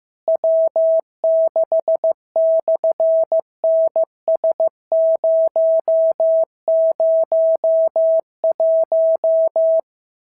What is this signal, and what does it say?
Morse code, heard as a keyed tone, about 15 wpm, 650 Hz. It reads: W6/NS001
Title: SOTA QSO Practice 15wpm